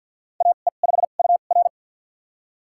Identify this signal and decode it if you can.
Morse code, heard as a keyed tone, about 52 wpm, spaced as for 36 wpm, 700 Hz. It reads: AE5UR
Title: USA Call Signs - 5 Characters - 2x Character Spacing - 50wpm